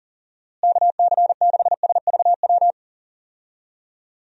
Morse code, heard as a keyed tone, about 40 wpm, 700 Hz. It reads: KC6SVW
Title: USA Call Signs - 6 Characters - Repeated 2x - 40wpm